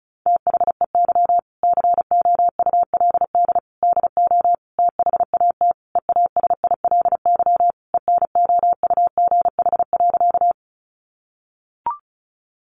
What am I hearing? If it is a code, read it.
Morse code, heard as a keyed tone, about 34 wpm, 700 Hz. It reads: THEY COULD DO THAT EASILY ENOUGH.
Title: Sentences from Top 900 Words 35wpm